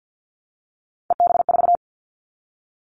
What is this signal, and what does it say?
Morse code, heard as a keyed tone, about 50 wpm, 700 Hz. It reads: EB4
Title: Callsign Prefixes for Top 25 DXCC - 50wpm